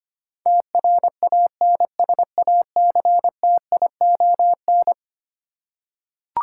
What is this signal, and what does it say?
Morse code, heard as a keyed tone, about 25 wpm, 700 Hz. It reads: TRANSACTION
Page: morse-code-ninja-practice-25wpm